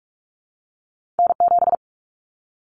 Morse code, heard as a keyed tone, about 46 wpm, 700 Hz. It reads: N7
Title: Common Cut Numbers - 2-Digit 45wpm